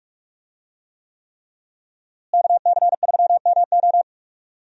Morse code, heard as a keyed tone, about 42 wpm, 700 Hz. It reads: KC3MO
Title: USA Call Signs - 5 Characters - Repeated 3x - 45wpm